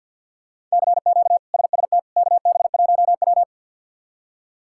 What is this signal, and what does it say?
Morse code, heard as a keyed tone, about 50 wpm, 700 Hz. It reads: CQ SST KB1W
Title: K1USN SST QSO Practice - 50wpm